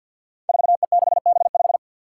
Morse code, heard as a keyed tone, about 52 wpm, 700 Hz. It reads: 4E6B5
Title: Five Challenging Letter-Number Combinations 50wpm